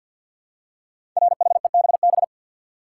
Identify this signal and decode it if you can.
Morse code, heard as a keyed tone, about 49 wpm, 700 Hz. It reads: RHEBB
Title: Code Groups - 5 Letters - 50wpm